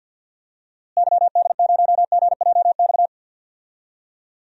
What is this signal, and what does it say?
Morse code, heard as a keyed tone, about 49 wpm, 700 Hz. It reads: YD0GJX
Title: DXCC-100 Call Signs - 6 Characters - Repeated 2x - 50wpm